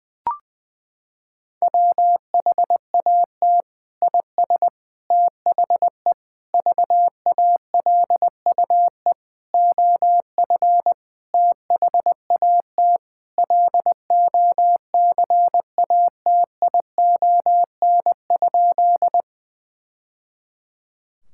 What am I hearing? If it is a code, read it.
Morse code, heard as a keyed tone, about 21 wpm, 700 Hz. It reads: WHAT IS THE VALUE OF THAT LOCATION?
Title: Sentences from Top 1000 Words - Review Part 6 - 20wpm